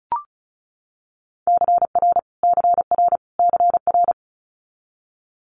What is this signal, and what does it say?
Morse code, heard as a keyed tone, about 36 wpm, 700 Hz. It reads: CR CR CR
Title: Multi-letter Phonemes Repeated 3x 35wpm